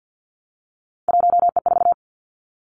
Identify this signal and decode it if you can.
Morse code, heard as a keyed tone, about 50 wpm, 700 Hz. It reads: 1E4
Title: Three Challenging Letter-Number Combinations 50wpm